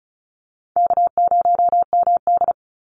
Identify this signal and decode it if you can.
Morse code, heard as a keyed tone, about 34 wpm, 700 Hz. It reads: K0MD
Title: Call Signs - 35wpm